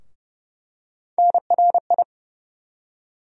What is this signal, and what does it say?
Morse code, heard as a keyed tone, about 30 wpm, 700 Hz. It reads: NRI
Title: USA Call Sign Postfixes - 3 Letters 30wpm